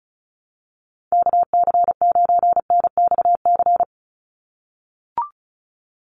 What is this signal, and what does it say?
Morse code, heard as a keyed tone, about 34 wpm, 700 Hz. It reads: KC9NXC
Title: USA Call Signs 35wpm